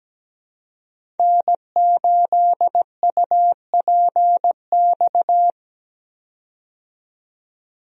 Morse code, heard as a keyed tone, about 17 wpm, 700 Hz. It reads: N8UPX
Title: USA Call Signs - 5 Characters - 17wpm